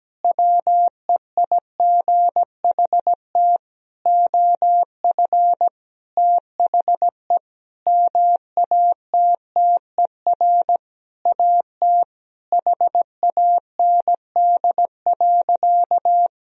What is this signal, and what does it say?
Morse code, heard as a keyed tone, about 17 wpm, 700 Hz. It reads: WEIGHT OF THE MATTER AT HAND.
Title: Sentences from Top 1000 Words - Review Part 3 - 17wpm